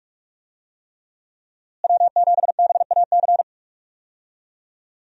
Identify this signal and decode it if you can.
Morse code, heard as a keyed tone, about 46 wpm, 700 Hz. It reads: W7BAC
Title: USA Call Signs - 5 Characters - Repeated 3x - 45wpm